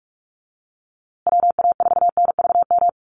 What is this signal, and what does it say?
Morse code, heard as a keyed tone, about 44 wpm, 700 Hz. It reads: WA4NVM
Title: USA Call Signs Repeated 2x 45wpm